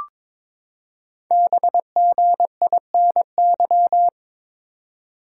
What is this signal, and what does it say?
Morse code, heard as a keyed tone, about 22 wpm, 700 Hz. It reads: BGINY
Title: Code Groups - 5 Letters - 22wpm